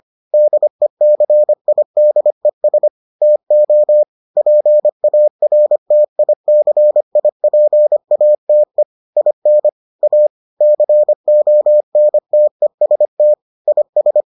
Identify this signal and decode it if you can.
Morse code, heard as a keyed tone, about 25 wpm, 600 Hz. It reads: DECIDES TO PARTICIPATE IN A CONTEST IS